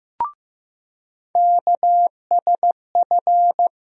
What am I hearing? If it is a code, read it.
Morse code, heard as a keyed tone, about 15 wpm, 700 Hz. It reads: KSF
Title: USA Call Sign Postfixes - 3 Letters 15wpm